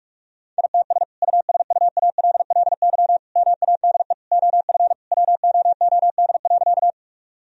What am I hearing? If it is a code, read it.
Morse code, heard as a keyed tone, about 46 wpm, 700 Hz. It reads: ITS USUALLY MADE OF WOOD.
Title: Sentences from Top 1000 Words - Review Part 3 - 45wpm